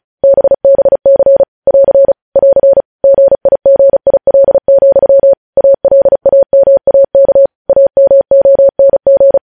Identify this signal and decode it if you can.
Morse code, heard as a keyed tone, about 36 wpm, 550 Hz. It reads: BBC <AR> <AR> GIGIL, ALAMAK AMONG
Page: News headlines in Morse code at 35 wpm